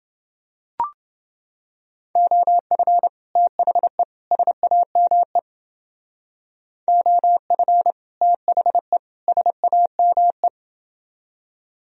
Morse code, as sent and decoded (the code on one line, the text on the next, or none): --- ..-. / - .... / ... .- -- . / --- ..-. / - .... / ... .- -- .
OF THE SAME OF THE SAME